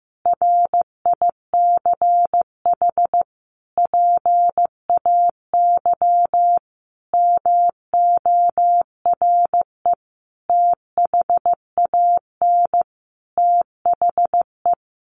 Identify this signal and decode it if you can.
Morse code, heard as a keyed tone, about 15 wpm, 700 Hz. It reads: RICH PAY MORE THAN THE
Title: Sentences from Top 1100 Words 15wpm